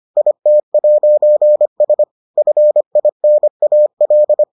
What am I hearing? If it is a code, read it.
Morse code, heard as a keyed tone, about 25 wpm, 600 Hz. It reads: IT'S FINAL